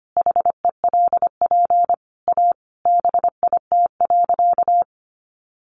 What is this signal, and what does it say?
Morse code, heard as a keyed tone, about 25 wpm, 700 Hz. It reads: HELP A BIT.
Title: Sentences from Top 900 Words 25wpm